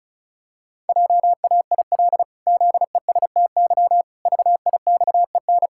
Morse code, heard as a keyed tone, about 34 wpm, 700 Hz. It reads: JAIL ZESTY VIXEN
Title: Pangram Challenge v2 - 35wpm